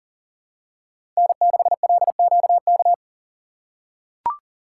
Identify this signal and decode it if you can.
Morse code, heard as a keyed tone, about 40 wpm, 700 Hz. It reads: N6LQK
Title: USA Call Signs - 5 Characters - 4x Character Spacing - 40wpm